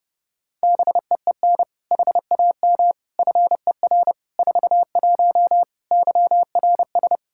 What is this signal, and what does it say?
Morse code, heard as a keyed tone, about 32 wpm, 700 Hz. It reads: BEEN HAM FER 41 YRS
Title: Practice QSO - Exchange 3 - More Info - 30wpm